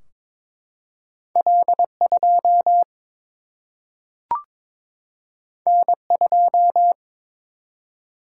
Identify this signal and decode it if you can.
Morse code, heard as a keyed tone, about 22 wpm, 700 Hz. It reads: L 2 N 2
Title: Single-Letter Single-Number 22wpm